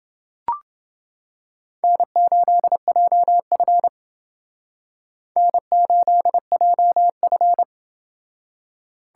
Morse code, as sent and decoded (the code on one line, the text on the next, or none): -. ---.. .--- ..-. / -. ---.. .--- ..-.
N8JF N8JF